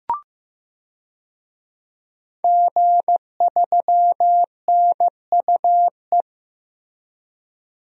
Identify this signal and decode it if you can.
Morse code, heard as a keyed tone, about 15 wpm, 700 Hz. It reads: G3NUE
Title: DXCC-100 Call Signs - 5 Characters - 15wpm